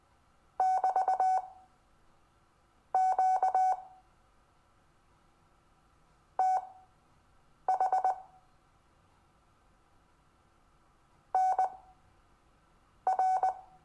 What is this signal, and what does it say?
Morse code, heard as a keyed tone, about 19 wpm, spaced as for 5 wpm, 750 Hz. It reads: <BT>QTH NR
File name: cw / Lesson22 REVIEW